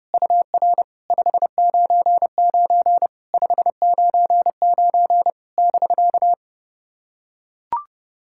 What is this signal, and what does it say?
Morse code, heard as a keyed tone, about 30 wpm, 700 Hz. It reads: UR 599 599 <BK>
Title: SOTA QSO Practice 30wpm